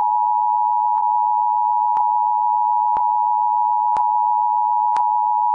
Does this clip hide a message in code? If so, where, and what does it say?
Yes: Emergency Alert System (SAME) tones (the 853 and 960 Hz attention tone) from 1.0 to 5.5 s.